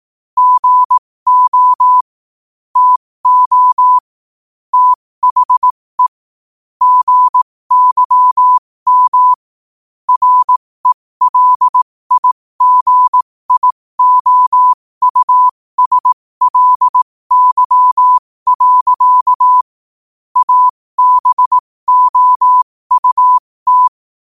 Morse code, heard as a keyed tone, about 18 wpm, spaced as for 15 wpm, 1000 Hz. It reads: GO TO THE GYM RELIGIOUSLY. ABOUT
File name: qotdfiles / 15wpm